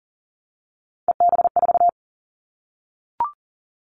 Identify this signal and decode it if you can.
Morse code, heard as a keyed tone, about 40 wpm, 700 Hz. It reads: EB4
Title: Callsign Prefixes for Top 25 DXCC - Repeated 3x - 40wpm